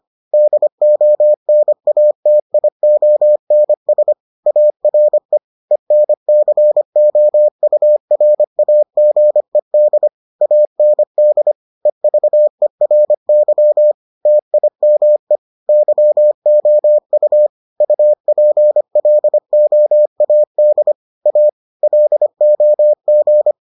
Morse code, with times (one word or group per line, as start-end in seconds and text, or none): 0.3-4.1 DONATIONS
4.5-5.4 ARE
5.7-10.1 ENCOURAGED
10.4-11.5 AND
11.9-13.9 EVERY
14.3-15.4 TIME
15.7-17.5 YOU
17.8-20.9 UPLOAD
21.3-21.5 A
21.8-23.5 LOG